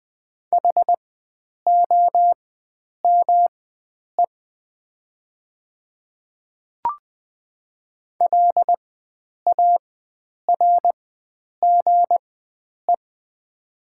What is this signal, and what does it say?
Morse code, heard as a keyed tone, about 20 wpm, spaced as for 9 wpm, 700 Hz. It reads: HOME LARGE
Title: 4x Character Spacing - Top 500 Words - 20wpm